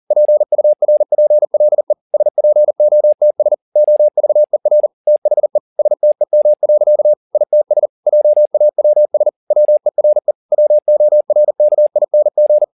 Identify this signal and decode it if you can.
Morse code, heard as a keyed tone, about 40 wpm, 600 Hz. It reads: PURPLE SPOTS OVER THE STEM. ITS JAWS WERE WORKING
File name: the old man and the sea 2 40wpm